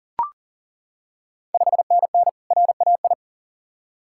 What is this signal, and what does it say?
Morse code, heard as a keyed tone, about 40 wpm, 700 Hz. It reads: 5NN RAI
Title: Ontario QSO Party Exchange 40wpm